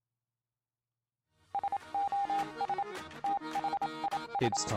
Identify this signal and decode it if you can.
Morse code, heard as a keyed tone, about 27 wpm, 800 Hz, with speech. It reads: SOS THIS